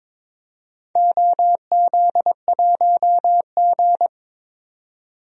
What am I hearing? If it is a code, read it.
Morse code, heard as a keyed tone, about 21 wpm, 700 Hz. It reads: OZ1G
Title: DXCC-100 Call Signs - 4 Characters - 22wpm